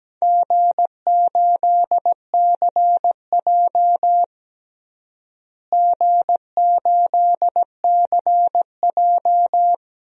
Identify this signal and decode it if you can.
Morse code, heard as a keyed tone, about 17 wpm, 700 Hz. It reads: G8CJ G8CJ
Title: DXCC-100 Call Signs - 4 Characters - Repeated 2x - 17wpm